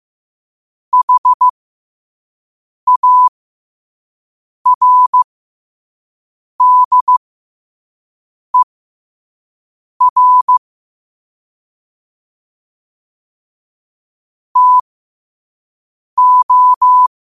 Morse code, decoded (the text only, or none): HARDER TO